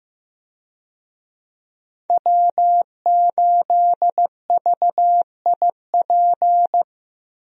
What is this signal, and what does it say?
Morse code, heard as a keyed tone, about 15 wpm, 700 Hz. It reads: W8VIP
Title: USA Call Signs - 5 Characters - Repeated 2x - 15wpm